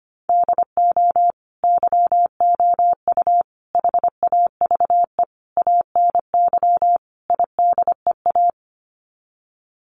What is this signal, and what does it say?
Morse code, heard as a keyed tone, about 25 wpm, 700 Hz. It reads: DO YOU HAVE ANY IDEA
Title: Sets of 5 Words Encore 25wpm